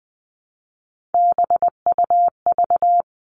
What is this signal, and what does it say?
Morse code, heard as a keyed tone, about 20 wpm, 700 Hz. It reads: BUV
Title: All Cut Numbers - 3-Digit 20wpm